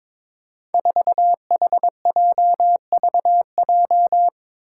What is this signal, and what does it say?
Morse code, heard as a keyed tone, about 22 wpm, 700 Hz. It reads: 4HJVJ
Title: Five Challenging Letter-Number Combinations 22wpm